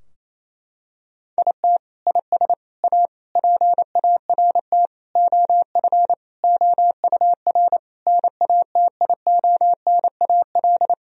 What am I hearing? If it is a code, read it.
Morse code, heard as a keyed tone, about 28 wpm, 700 Hz. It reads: IT IS A PART OF OUR NATIONAL
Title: Sentences from Top 1000 Words - Review Part 4 - 28wpm